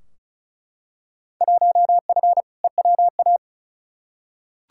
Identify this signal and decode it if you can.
Morse code, heard as a keyed tone, about 36 wpm, 700 Hz. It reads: 1F EWA
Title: Field Day Exchanges Encore - 35wpm